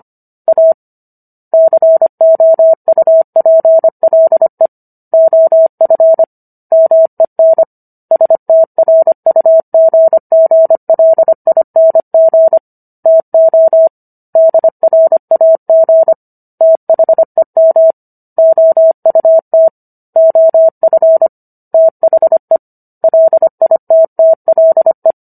Morse code, heard as a keyed tone, about 25 wpm, 650 Hz. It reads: A COUPLE OF MEN STRUGGLING TO DRAG THEM OUT OF THE LITTLE
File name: Book One / 16-Chapter 16 - 25 WPM